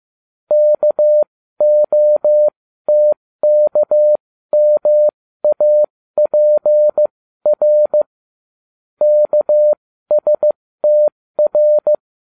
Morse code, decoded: KOTKMAPR KSTR

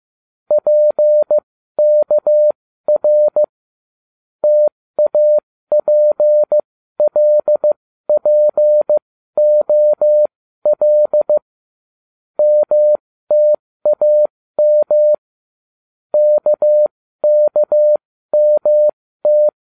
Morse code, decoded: PKR TAPLPOL MTAM KKMT

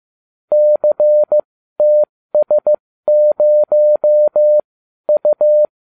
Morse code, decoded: CTS0U

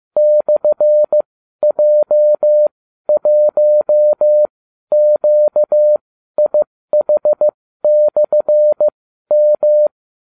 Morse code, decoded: /J1QIH/M

